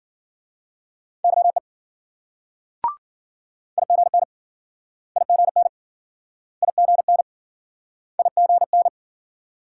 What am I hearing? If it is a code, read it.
Morse code, heard as a keyed tone, about 53 wpm, 700 Hz. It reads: YE IGN IGN IGN IGN